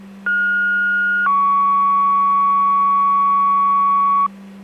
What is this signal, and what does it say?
Two-tone page, heard as a one-second tone, then a three-second tone, about 1430 then 1120 Hz. It is heard from 0.3 to 4.3 s.